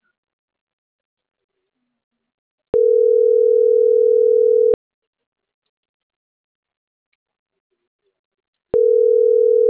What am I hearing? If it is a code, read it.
Ringback tone (North America), 440 + 480 Hz.